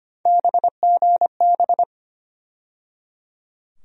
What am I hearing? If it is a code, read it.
Morse code, heard as a keyed tone, about 25 wpm, 700 Hz. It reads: BGB
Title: USA Call Sign Postfixes - 3 Letters 25wpm